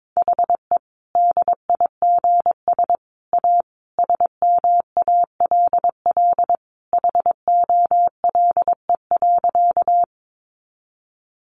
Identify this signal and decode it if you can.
Morse code, heard as a keyed tone, about 22 wpm, 700 Hz. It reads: HE DIGS A SMALL HOLE.
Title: Easy Phrases 22wpm